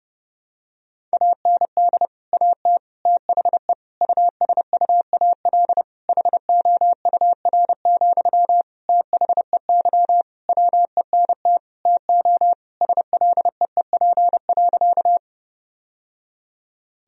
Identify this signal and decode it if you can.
Morse code, heard as a keyed tone, about 30 wpm, 700 Hz. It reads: AND AT THE USUAL HOUR, THEY WENT TO SLEEP.
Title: Sentences from Top 1000 Words - Review Part 2 - 30wpm